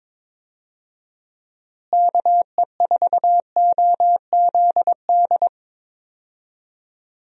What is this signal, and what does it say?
Morse code, heard as a keyed tone, about 21 wpm, 700 Hz. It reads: KE4OZD
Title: USA Call Signs - 6 Characters - Repeated 3x - 22wpm